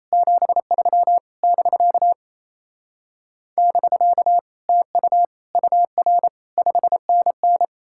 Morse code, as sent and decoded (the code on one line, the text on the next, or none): --... ...-- / -...-.- / -...-.- / - ..- / ..- .-. / ..... -. -.
73 <BK> <BK> TU UR 5NN